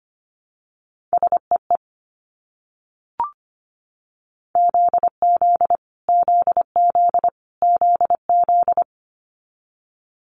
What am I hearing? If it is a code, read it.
Morse code, heard as a keyed tone, about 25 wpm, 700 Hz. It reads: SEE ZZ ZZ ZZ
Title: Multi-letter Phonemes Repeated 3x 25wpm